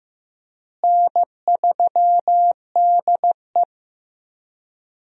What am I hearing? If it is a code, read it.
Morse code, heard as a keyed tone, about 15 wpm, 700 Hz. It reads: N3DE
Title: USA Call Signs - 4 Characters - Repeated 3x - 15wpm